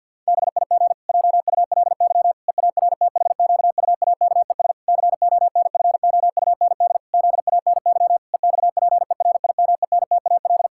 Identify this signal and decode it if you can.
Morse code, heard as a keyed tone, about 51 wpm, 700 Hz. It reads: BIG JULY EARTHQUAKES CONFOUND ZANY EXPERIMENTAL
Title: Pangram Challenge v2 - 50wpm